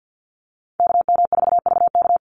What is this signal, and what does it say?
Morse code, heard as a keyed tone, about 48 wpm, 700 Hz. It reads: KM4VK
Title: USA Call Signs 50wpm